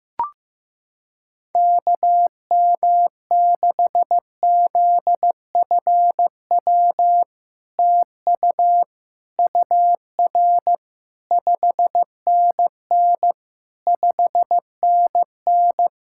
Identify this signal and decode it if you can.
Morse code, heard as a keyed tone, about 15 wpm, 700 Hz. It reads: KM6ZFW TU UR 5NN 5NN
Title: SOTA QSO Practice 15wpm